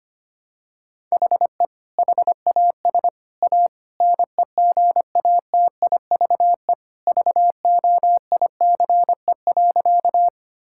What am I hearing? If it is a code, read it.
Morse code, heard as a keyed tone, about 25 wpm, 700 Hz. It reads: HE HAS A NEGATIVE VOICE.
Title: Sentences from Top 1000 Words - Review Part 6 - 25wpm